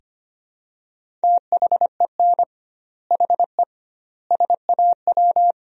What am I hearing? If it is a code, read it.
Morse code, heard as a keyed tone, about 25 wpm, 700 Hz. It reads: THEN HE SAW
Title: 2x Word-Spacing - Sentences from Top 400 Words - 25wpm